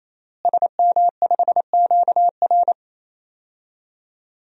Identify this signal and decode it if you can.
Morse code, heard as a keyed tone, about 28 wpm, 700 Hz. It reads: SM5QR
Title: DXCC-100 Call Signs - 5 Characters - Repeated 3x - 28wpm